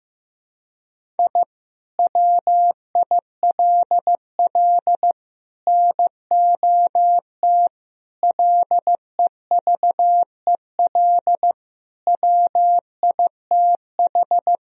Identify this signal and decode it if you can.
Morse code, heard as a keyed tone, about 15 wpm, 700 Hz. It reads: I WILL NOT LEVEL WITH
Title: Sentences from Top 1000 Words - Review Part 4 - 15wpm